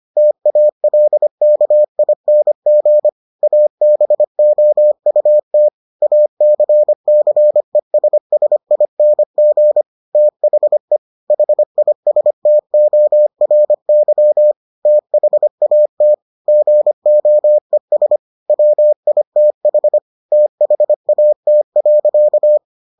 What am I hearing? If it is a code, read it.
Morse code, heard as a keyed tone, about 25 wpm, 600 Hz. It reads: TALKING ABOUT ACCESSING THE HISTORY THAT GOES WITH THAT.